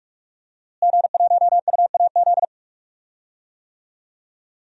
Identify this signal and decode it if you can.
Morse code, heard as a keyed tone, about 46 wpm, 700 Hz. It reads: G1UAZ
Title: DXCC-100 Call Signs - 5 Characters - Repeated 2x - 45wpm